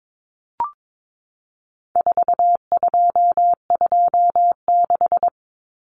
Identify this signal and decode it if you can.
Morse code, heard as a keyed tone, about 22 wpm, 700 Hz. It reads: 4226
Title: Common Cut Numbers - 4-Digit 22wpm